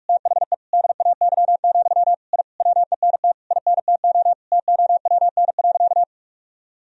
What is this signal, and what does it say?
Morse code, heard as a keyed tone, about 41 wpm, 700 Hz. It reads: THE DAY, I WENT INTO TOWN.
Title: Sentences from Top 1000 Words - Review Part 2 - 45wpm